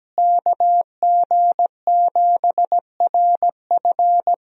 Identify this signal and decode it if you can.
Morse code, heard as a keyed tone, about 17 wpm, 700 Hz. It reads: KG7RF
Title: USA Call Signs - 5 Characters - Repeated 3x - 17wpm